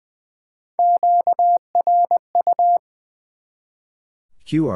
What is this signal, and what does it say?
Morse code, heard as a keyed tone, about 20 wpm, 700 Hz. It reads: QRU
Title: Top 90 QSO Elements 20wpm